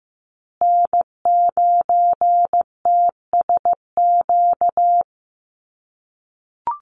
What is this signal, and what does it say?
Morse code, heard as a keyed tone, about 15 wpm, 700 Hz. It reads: N9TSQ
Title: USA Call Signs Repeated 2x 15wpm